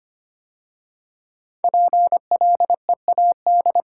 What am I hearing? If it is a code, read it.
Morse code, heard as a keyed tone, about 25 wpm, 700 Hz. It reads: PLEAD
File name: 5 Letter Words - First 3 Letters Given - 25wpm